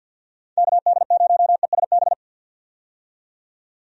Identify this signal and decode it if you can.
Morse code, heard as a keyed tone, about 51 wpm, 700 Hz. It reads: KD0ESB
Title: USA Call Signs - 6 Characters - 50wpm